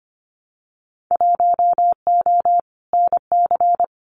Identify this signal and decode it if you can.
Morse code, heard as a keyed tone, about 25 wpm, 700 Hz. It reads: 1O NC
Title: Winter Field Day Exchanges 25wpm